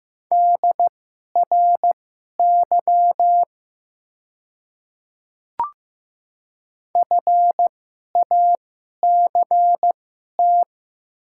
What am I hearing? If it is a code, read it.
Morse code, heard as a keyed tone, about 15 wpm, spaced as for 11 wpm, 700 Hz. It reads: DRY FACT
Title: 2x Character Spacing - Top 500 Words - 15wpm